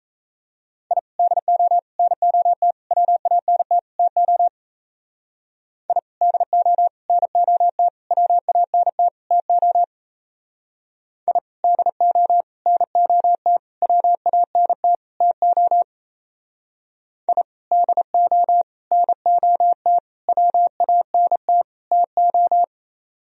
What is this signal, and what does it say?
Morse code, heard as a keyed tone, about 34 wpm, 700 Hz. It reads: I DO NOT WANT TO I DO NOT WANT TO I DO NOT WANT TO I DO NOT WANT TO
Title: Speed-Racing - Sets of 5 Words 28wpm